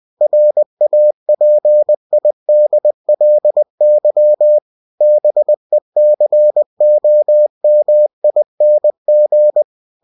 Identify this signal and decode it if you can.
Morse code, heard as a keyed tone, about 20 wpm, 600 Hz. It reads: RAPIDLY BECOMING
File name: cw-books / winnie-the-pooh / winnie-20wpm-page-0097